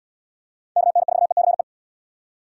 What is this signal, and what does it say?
Morse code, heard as a keyed tone, about 76 wpm, 700 Hz. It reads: KN4ECE